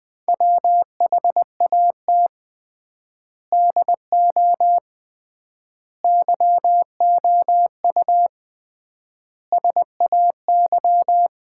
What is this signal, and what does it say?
Morse code, heard as a keyed tone, about 21 wpm, 700 Hz. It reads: WHAT DO YOU SAY
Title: 3x Word-Spacing - Sets of 5 Words - 20wpm